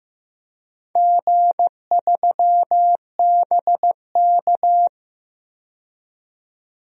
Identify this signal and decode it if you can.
Morse code, heard as a keyed tone, about 15 wpm, 700 Hz. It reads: G3BK
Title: DXCC-100 Call Signs - 4 Characters - 15wpm